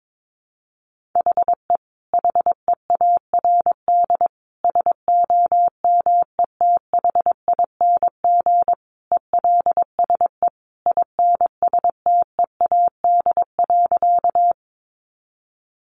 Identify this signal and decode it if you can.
Morse code, heard as a keyed tone, about 22 wpm, 700 Hz. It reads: HE HEARD SOMETHING ELSE INSTEAD.
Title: Sentences from Top 800 Words 22wpm